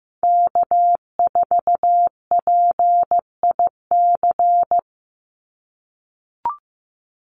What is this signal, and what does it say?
Morse code, heard as a keyed tone, about 15 wpm, 700 Hz. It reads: K4PIC
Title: NA SOTA Chaser Callsigns 15wpm